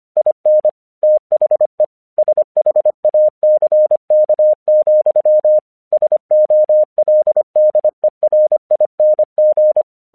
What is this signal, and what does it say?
Morse code, heard as a keyed tone, about 25 wpm, 610 Hz. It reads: IN THE SHACK, SOLDERING
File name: English 25wpm